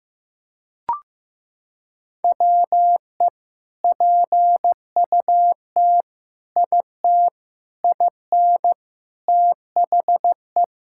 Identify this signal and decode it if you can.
Morse code, heard as a keyed tone, about 15 wpm, 700 Hz. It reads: WE PUT IT IN THE
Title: Sentences from Top 1000 Words - Review Part 5 - 15wpm